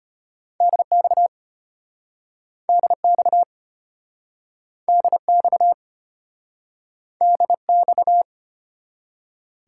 Morse code, heard as a keyed tone, about 32 wpm, 700 Hz. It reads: DX DX DX DX